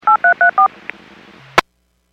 DTMF, heard as telephone touch tones, tones of about 110 ms. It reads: 8334
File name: WPS / 01894 18-41 WPS 451.25 WPS